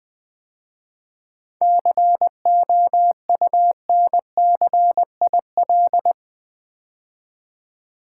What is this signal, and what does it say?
Morse code, heard as a keyed tone, about 20 wpm, 700 Hz. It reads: COUNCIL
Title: 7 Letter Words - First 3 Letters Given - 20wpm